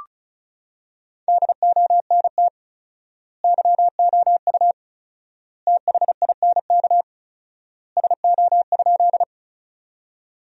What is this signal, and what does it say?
Morse code, heard as a keyed tone, about 34 wpm, 700 Hz. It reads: DONT YOU THINK SO?
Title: 4x Word-Spacing - Sentences from Top 300 Words - 35wpm